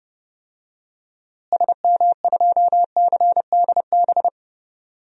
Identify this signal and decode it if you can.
Morse code, heard as a keyed tone, about 30 wpm, 700 Hz. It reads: SM2CDB
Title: DXCC-100 Call Signs - 6 Characters - Repeated 2x - 30wpm